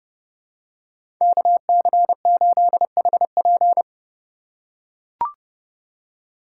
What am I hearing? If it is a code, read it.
Morse code, heard as a keyed tone, about 32 wpm, 700 Hz. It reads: KC8HP
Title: USA Call Signs - 5 Characters - 3x Character Spacing - 30wpm